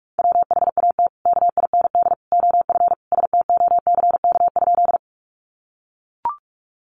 Morse code, heard as a keyed tone, about 44 wpm, 700 Hz. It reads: WHAT KIND OF STOCK?
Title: Sentences from Top 800 Words 45wpm